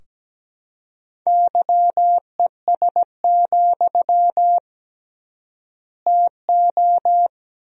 Morse code, heard as a keyed tone, about 17 wpm, 700 Hz. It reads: YES, TO
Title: 3x Word-Spacing - Sentences from Top 500 Words - 17wpm